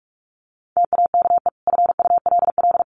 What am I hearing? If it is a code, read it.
Morse code, heard as a keyed tone, about 44 wpm, 700 Hz. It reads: TAKE FULL
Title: Sentences from Top 1400 Words 45wpm